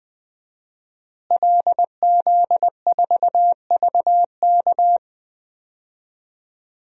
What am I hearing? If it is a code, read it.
Morse code, heard as a keyed tone, about 20 wpm, 700 Hz. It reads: LZ4VK
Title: DXCC-100 Call Signs - 5 Characters - Repeated 3x - 20wpm